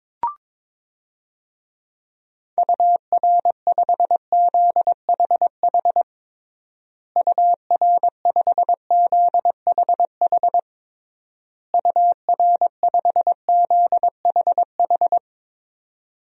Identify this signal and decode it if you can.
Morse code, heard as a keyed tone, about 22 wpm, 700 Hz. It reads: UR5ZHH UR5ZHH UR5ZHH
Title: DXCC-100 Call Signs - 6 Characters - Repeated 3x - 22wpm